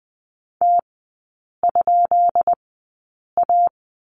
Morse code, heard as a keyed tone, about 20 wpm, spaced as for 8 wpm, 700 Hz. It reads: T?A
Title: 152 Lesson 25e - Letter p - Three Characters 20wpm